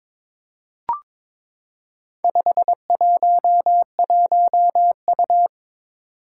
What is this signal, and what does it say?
Morse code, heard as a keyed tone, about 22 wpm, 700 Hz. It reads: 511U
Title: Four Challenging Letter-Number Combinations 22wpm